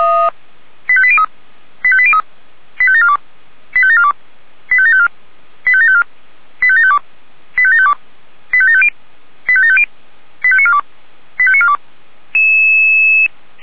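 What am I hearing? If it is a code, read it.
Five-tone selective call (ZVEI), heard as quick runs of five tones, tones of about 70 ms. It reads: 85803 85803 85742 85742 85754 85754 85752 85752 85750 85750 85942 85942
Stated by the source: ZVEI five-tone selective call received on SDR.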